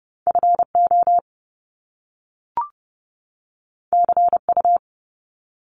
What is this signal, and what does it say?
Morse code, heard as a keyed tone, about 30 wpm, 700 Hz. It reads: FO CU
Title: Top Two Letter Combinations in Words 30wpm